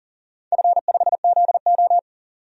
Morse code, heard as a keyed tone, about 40 wpm, 700 Hz. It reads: F5ZO
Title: DXCC-100 Call Signs - 4 Characters - 40wpm